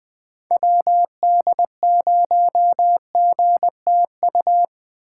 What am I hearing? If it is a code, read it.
Morse code, heard as a keyed tone, about 20 wpm, 700 Hz. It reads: WD0GTU
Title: USA Call Signs - 6 Characters - 2x Character Spacing - 20wpm